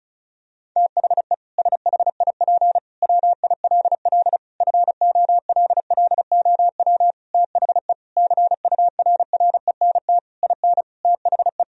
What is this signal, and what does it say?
Morse code, heard as a keyed tone, about 36 wpm, 700 Hz. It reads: THE SHIP WILL FOLLOW THE CURRENT IN THE
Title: Sentences from Top 1000 Words - Review Part 4 - 35wpm